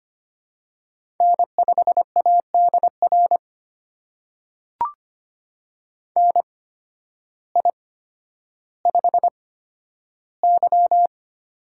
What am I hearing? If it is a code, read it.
Morse code, heard as a keyed tone, about 25 wpm, spaced as for 12 wpm, 700 Hz. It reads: N5ADR NI5Y
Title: USA Call Signs - 8x Character Spacing - 25wpm